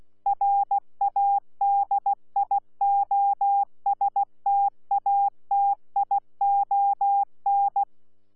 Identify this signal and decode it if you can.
Morse code, heard as a keyed tone, about 16 wpm, 800 Hz. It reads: RADIOSTATION
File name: Z PI4VRZ / PI4VRZ 14-jun-2025